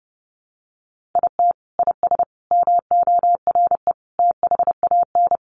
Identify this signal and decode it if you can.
Morse code, heard as a keyed tone, about 30 wpm, 700 Hz. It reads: IT IS MORE THAN
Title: Sentences from Top 1400 Words 30wpm